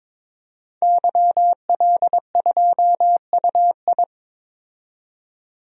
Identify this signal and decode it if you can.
Morse code, heard as a keyed tone, about 22 wpm, 700 Hz. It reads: YL2UI
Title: DXCC-100 Call Signs - 5 Characters - 22wpm